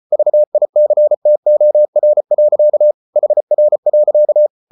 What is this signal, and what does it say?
Morse code, heard as a keyed tone, about 34 wpm, 600 Hz. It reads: VICTOR. HR.
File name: Ferie QRQ 34wpm